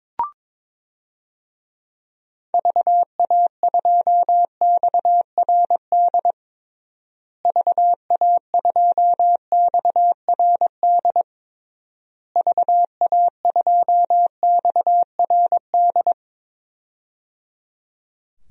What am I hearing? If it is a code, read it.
Morse code, heard as a keyed tone, about 22 wpm, 700 Hz. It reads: VA2XRD VA2XRD VA2XRD
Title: DXCC-100 Call Signs - 6 Characters - Repeated 3x - 22wpm